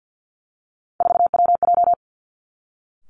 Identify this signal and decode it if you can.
Morse code, heard as a keyed tone, about 50 wpm, 700 Hz. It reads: 4WJ
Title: Three Challenging Letter-Number Combinations 50wpm